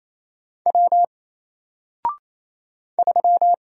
Morse code, heard as a keyed tone, about 27 wpm, 700 Hz. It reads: W 3
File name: Single-Letter-Number-28wpm